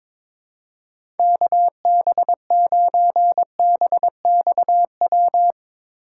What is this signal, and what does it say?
Morse code, heard as a keyed tone, about 22 wpm, 700 Hz. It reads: KB9BXW
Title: USA Call Signs - 6 Characters - 2x Character Spacing - 22wpm